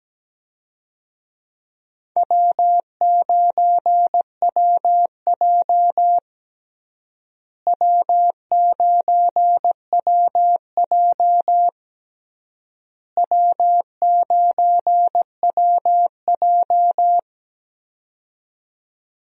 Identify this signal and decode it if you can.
Morse code, heard as a keyed tone, about 17 wpm, 700 Hz. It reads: W9WJ W9WJ W9WJ
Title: USA Call Signs - 4 Characters - Repeated 3x - 17wpm